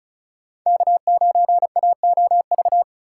Morse code, heard as a keyed tone, about 36 wpm, 700 Hz. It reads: K9AOV
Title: USA Call Signs - 4x Character Spacing - 35wpm